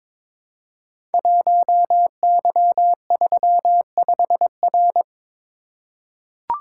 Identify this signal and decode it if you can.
Morse code, heard as a keyed tone, about 22 wpm, 700 Hz. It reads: 1Y35R
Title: Code Groups - 5 Letter:Numbers - 22wpm